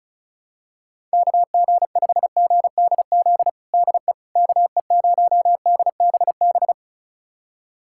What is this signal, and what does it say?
Morse code, heard as a keyed tone, about 34 wpm, 700 Hz. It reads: KG5GDZ DE KE0DBB
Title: USA Callsign DE Callsign - 6 Characters - 35wpm